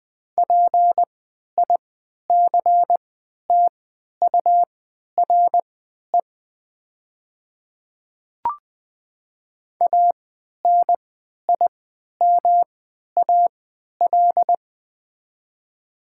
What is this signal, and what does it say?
Morse code, heard as a keyed tone, about 20 wpm, spaced as for 11 wpm, 700 Hz. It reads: PICTURE ANIMAL
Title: 3x Character Spacing - Top 200 Words - 20wpm